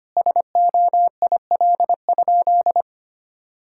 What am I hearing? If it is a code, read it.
Morse code, heard as a keyed tone, about 25 wpm, 700 Hz. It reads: SOIL?
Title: Sentences from Top 1600 Words 25wpm